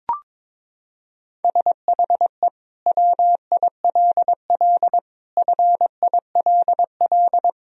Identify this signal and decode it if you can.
Morse code, heard as a keyed tone, about 22 wpm, 700 Hz. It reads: SHE WILL FILL
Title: Sentences from Top 1000 Words - Review Part 3 - 22wpm